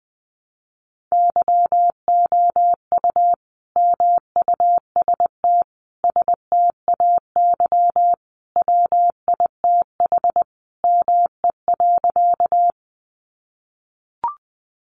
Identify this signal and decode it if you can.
Morse code, heard as a keyed tone, about 20 wpm, 700 Hz. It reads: YOU MUST STAY WITH ME.
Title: Sentences from Top 500 Words 20wpm